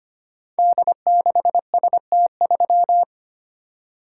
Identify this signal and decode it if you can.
Morse code, heard as a keyed tone, about 25 wpm, 700 Hz. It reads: D6ST3
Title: Code Groups - 5 Letter:Numbers - 25wpm